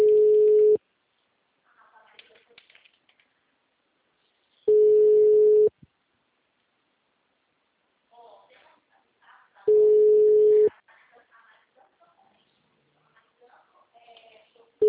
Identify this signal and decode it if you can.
Ringback tone (Europe, 425 Hz), with speech.